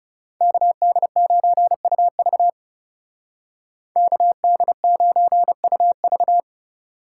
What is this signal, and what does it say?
Morse code, heard as a keyed tone, about 39 wpm, 700 Hz. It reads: KD9UV KD9UV